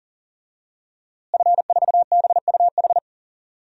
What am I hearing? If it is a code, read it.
Morse code, heard as a keyed tone, about 40 wpm, 700 Hz. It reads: F4BUH
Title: DXCC-100 Call Signs - 5 Characters - Repeated 3x - 40wpm